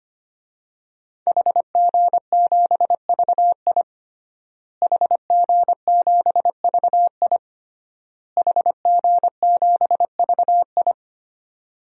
Morse code, heard as a keyed tone, about 25 wpm, 700 Hz. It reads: HG7VI HG7VI HG7VI
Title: DXCC-100 Call Signs - 5 Characters - Repeated 3x - 25wpm